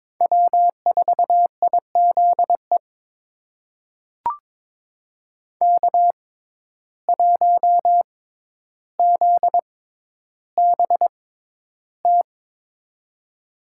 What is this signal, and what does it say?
Morse code, heard as a keyed tone, about 22 wpm, spaced as for 11 wpm, 700 Hz. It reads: W4IZE K1ZBT
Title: USA Call Signs - 5 Characters - 6x Character Spacing - 22wpm